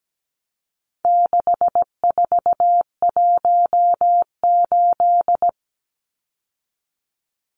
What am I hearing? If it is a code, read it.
Morse code, heard as a keyed tone, about 17 wpm, 700 Hz. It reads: 6418
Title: Numbers - 4-Digit 17wpm